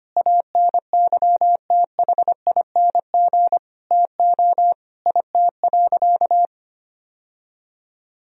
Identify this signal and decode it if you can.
Morse code, heard as a keyed tone, about 25 wpm, 700 Hz. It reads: ANYTHING TO IT.
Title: Sentences from Top 1000 Words - Review Part 1 - 25wpm